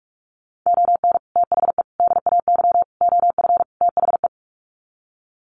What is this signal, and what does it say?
Morse code, heard as a keyed tone, about 44 wpm, 700 Hz. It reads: ON THE DAY OF THE
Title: Sets of 5 Words Encore 45wpm